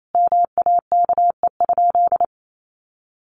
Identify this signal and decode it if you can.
Morse code, heard as a keyed tone, about 28 wpm, 700 Hz. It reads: MAKE?
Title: Sentences from Top 1400 Words 28wpm